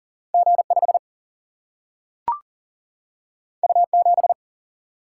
Morse code, heard as a keyed tone, about 40 wpm, 700 Hz. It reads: G 5 U 7
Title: Single-Letter Single-Number 40wpm